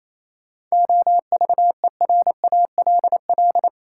Morse code, heard as a keyed tone, about 28 wpm, 700 Hz. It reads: OVERALL
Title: Sentences from Top 1000 Words - Review Part 7 - New Word Drill - 28wpm